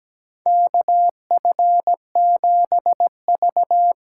Morse code, heard as a keyed tone, about 17 wpm, 700 Hz. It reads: KF7V
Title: USA Call Signs - 6 Characters - 8x Character Spacing - 17wpm